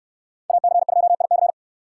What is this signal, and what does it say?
Morse code, heard as a keyed tone, about 64 wpm, 700 Hz. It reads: AC2IC